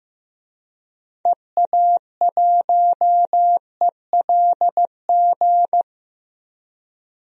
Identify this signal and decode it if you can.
Morse code, heard as a keyed tone, about 15 wpm, 700 Hz. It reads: EA1ELG
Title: DXCC-100 Call Signs - 6 Characters - 15wpm